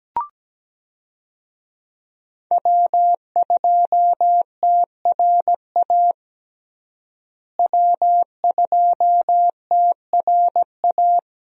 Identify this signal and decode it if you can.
Morse code, heard as a keyed tone, about 17 wpm, 700 Hz. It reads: W2TRA W2TRA
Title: USA Call Signs - 5 Characters - Repeated 3x - 17wpm